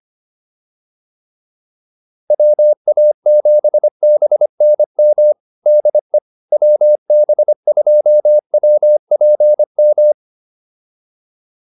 Morse code, heard as a keyed tone, about 25 wpm, 600 Hz. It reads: WA7BNM DE WB2WPM
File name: QSO106 25